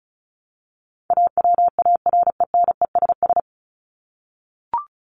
Morse code, heard as a keyed tone, about 34 wpm, 700 Hz. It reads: AWARENESS
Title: Top 1501-1750 Words 35wpm